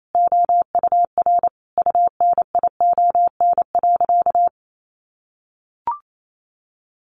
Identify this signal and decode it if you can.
Morse code, heard as a keyed tone, about 28 wpm, 700 Hz. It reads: OUR UNION.
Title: Sentences from Top 1400 Words 28wpm